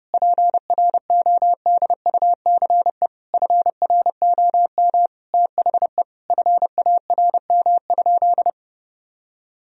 Morse code, heard as a keyed tone, about 32 wpm, 700 Hz. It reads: PRODUCE FROM THE FARM?
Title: Sentences from Top 1000 Words - Review Part 3 - 30wpm